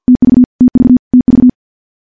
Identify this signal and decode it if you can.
Morse code, heard as a keyed tone, about 47 wpm, 250 Hz. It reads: TV TV TV
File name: morse / morse-training-50WPM